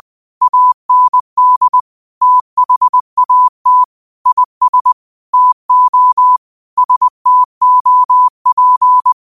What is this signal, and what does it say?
Morse code, heard as a keyed tone, about 20 wpm, 1000 Hz. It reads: AND THAT IS TO STOP